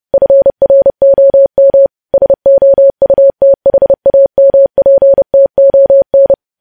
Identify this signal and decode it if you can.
Morse code, heard as a keyed tone, about 30 wpm, 550 Hz. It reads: FROM SOUTHAMPTON
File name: podcast / News headlines-1x-30x30